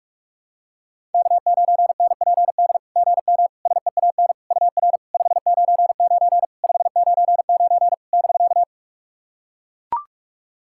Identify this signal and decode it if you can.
Morse code, heard as a keyed tone, about 44 wpm, 690 Hz. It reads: K9NPD GM SEAN UR 599 599 <BK>
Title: SOTA QSO Practice 45wpm